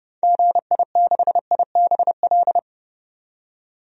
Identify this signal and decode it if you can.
Morse code, heard as a keyed tone, about 30 wpm, 700 Hz. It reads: GI6IBL
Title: DXCC-100 Call Signs - 6 Characters - 30wpm